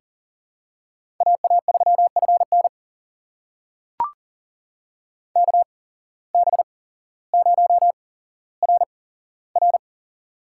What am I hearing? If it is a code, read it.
Morse code, heard as a keyed tone, about 40 wpm, spaced as for 17 wpm, 700 Hz. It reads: AA3FN KB0RR
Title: USA Call Signs - 8x Character Spacing - 40wpm